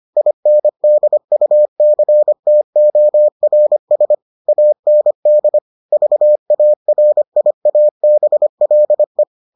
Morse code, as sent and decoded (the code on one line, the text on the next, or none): .. -. -.. ..- -.-. - --- .-. ... / .- -. -.. / ...- .- .-. .. .- -... .-.. .
INDUCTORS AND VARIABLE